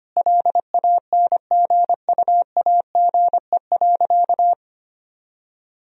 Morse code, heard as a keyed tone, about 25 wpm, 700 Hz. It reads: LANGUAGE.
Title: Sentences from Top 1000 Words - Review Part 3 - 25wpm